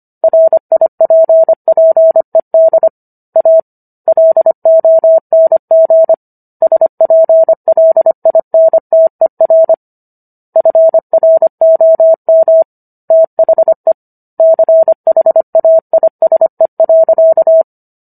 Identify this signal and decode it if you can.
Morse code, heard as a keyed tone, about 25 wpm, 650 Hz. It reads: RIPPED A LONG SPLINTER FROM THE CHAISE.